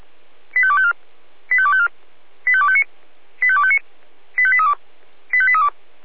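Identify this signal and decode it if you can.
Five-tone selective call (ZVEI), heard as quick runs of five tones, tones of about 70 ms. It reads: 85254 85254 85259 85259 85832 85832
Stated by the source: ZVEI five-tone call received on an SDR.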